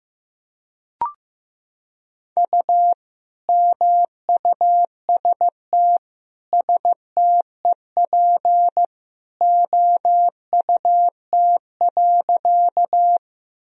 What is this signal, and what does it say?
Morse code, heard as a keyed tone, about 14 wpm, 700 Hz. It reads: U MUST STEP OUT.
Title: Sentences from Top 400 Words and Abbreviations 15wpm